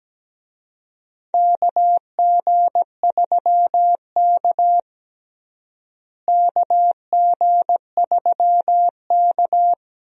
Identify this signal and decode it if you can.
Morse code, heard as a keyed tone, about 17 wpm, 700 Hz. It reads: KG3K KG3K
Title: USA Call Signs - 4 Characters - Repeated 2x - 17wpm